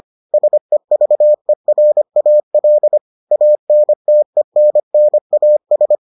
Morse code, heard as a keyed tone, about 25 wpm, 600 Hz. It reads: SEVERAL ANTENNAS